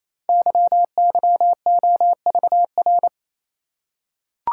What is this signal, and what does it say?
Morse code, heard as a keyed tone, about 29 wpm, 700 Hz. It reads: YYOVR
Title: Code Groups - 5 Letters - 28wpm